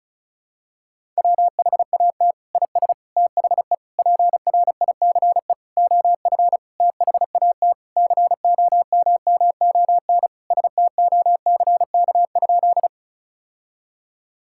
Morse code, heard as a keyed tone, about 36 wpm, 700 Hz. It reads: WHAT IS THE PRICE OF THAT COMMON STOCK?
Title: Sentences from Top 1000 Words - Review Part 5 - 35wpm